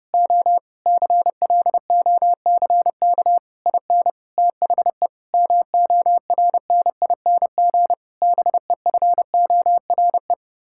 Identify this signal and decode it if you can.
Morse code, heard as a keyed tone, about 28 wpm, 700 Hz. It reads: O CLOCK IN THE MORNING BEFORE
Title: Pangram Challenge v2 - 30wpm